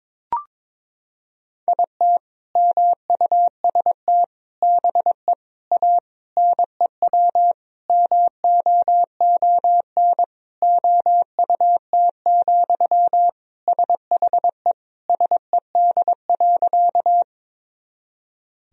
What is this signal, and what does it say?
Morse code, heard as a keyed tone, about 22 wpm, 700 Hz. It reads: IT MUST BE A NEW MOON OUT, SHE SED.
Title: Sentences from Top 500 Words and Abbreviations 22wpm